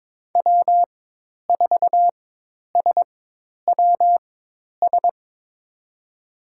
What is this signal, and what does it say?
Morse code, heard as a keyed tone, about 22 wpm, spaced as for 10 wpm, 700 Hz. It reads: W4SWS
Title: USA Call Signs - 4x Character Spacing - 22wpm